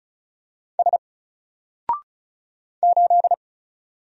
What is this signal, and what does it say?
Morse code, heard as a keyed tone, about 36 wpm, 700 Hz. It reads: S 8
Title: Single Character (Letter:Number:Punctuation:Prosign) Alt 35wpm